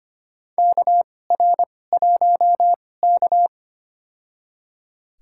Morse code, heard as a keyed tone, about 25 wpm, spaced as for 18 wpm, 700 Hz. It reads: KR1K
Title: USA Call Signs - 4 Characters - 2x Character Spacing - 25wpm